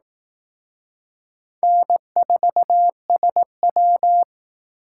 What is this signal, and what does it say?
Morse code, heard as a keyed tone, about 19 wpm, 700 Hz. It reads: N4SW
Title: Speed Racing - USA Call Signs 15wpm